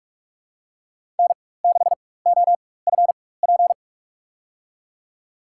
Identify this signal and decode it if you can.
Morse code, heard as a keyed tone, about 46 wpm, spaced as for 21 wpm, 700 Hz. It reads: N6OFP
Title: USA Call Signs - 5 Characters - 4x Character Spacing - 45wpm